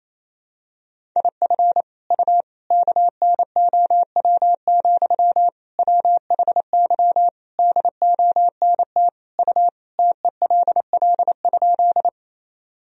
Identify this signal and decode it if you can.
Morse code, heard as a keyed tone, about 28 wpm, 700 Hz. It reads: IF U KNOW, WHY DONT U TELL?
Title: Sentences from Top 300 Words and Abbreviations 28wpm